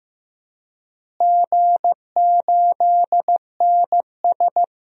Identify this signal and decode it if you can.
Morse code, heard as a keyed tone, about 15 wpm, 700 Hz. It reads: G8NS
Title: DXCC-100 Call Signs - 4 Characters - Repeated 2x - 15wpm